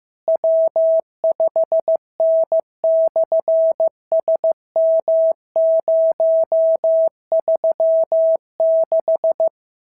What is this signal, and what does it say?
Morse code, heard as a keyed tone, about 15 wpm, 650 Hz. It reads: W5N/SM036
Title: SOTA QSO Practice 15wpm